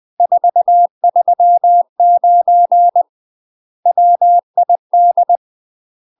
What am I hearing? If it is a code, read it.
Morse code, heard as a keyed tone, about 20 wpm, 700 Hz. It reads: 439 WID
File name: UK MASTER QSOs TRAINING / 20WPM / UK MASTER QSOs 0046